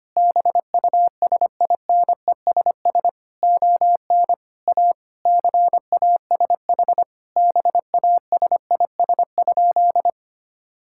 Morse code, heard as a keyed tone, about 25 wpm, 700 Hz. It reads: BUSINESS ON A CASH BASIS?
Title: Sentences from Top 1000 Words - Review Part 5 - 25wpm